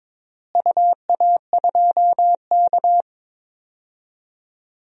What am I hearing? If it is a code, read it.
Morse code, heard as a keyed tone, about 22 wpm, 700 Hz. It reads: UA2K
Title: DXCC-100 Call Signs - 4 Characters - Repeated 3x - 22wpm